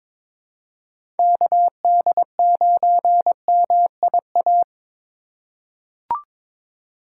Morse code, heard as a keyed tone, about 22 wpm, 700 Hz. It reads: KD9MIA
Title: USA Call Signs - 2x Character Spacing - 22wpm